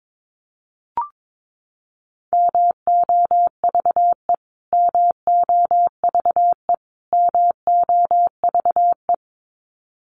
Morse code, heard as a keyed tone, about 22 wpm, 700 Hz. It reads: MOVE MOVE MOVE
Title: Top 300 Words Repeated 22wpm